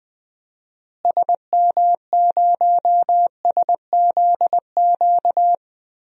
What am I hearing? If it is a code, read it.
Morse code, heard as a keyed tone, about 20 wpm, 700 Hz. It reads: SM0SZQ
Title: DXCC-100 Call Signs - 6 Characters - Repeated 3x - 20wpm